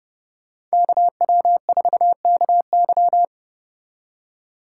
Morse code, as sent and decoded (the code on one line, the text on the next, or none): -.- .-- ....- -.- -.--
KW4KY